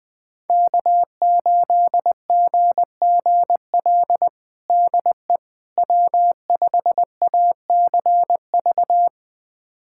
Morse code, heard as a keyed tone, about 20 wpm, 700 Hz. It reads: K8GGL DE W5ACV
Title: USA Callsign DE Callsign - 5 Characters - 20wpm